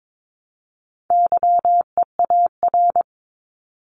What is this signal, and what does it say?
Morse code, heard as a keyed tone, about 22 wpm, 700 Hz. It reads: YEAR
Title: ES:S Suffix - Top 100 Words with Base Words 22wpm